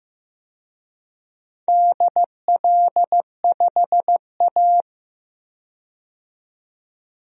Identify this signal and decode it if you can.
Morse code, heard as a keyed tone, about 15 wpm, 700 Hz. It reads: DL5A
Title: DXCC-100 Call Signs - 4 Characters - 15wpm